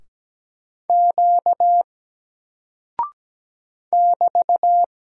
Morse code, heard as a keyed tone, about 17 wpm, 700 Hz. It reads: Q <BT>
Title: Single Character (Letter:Number:Punctuation:Prosign) 17wpm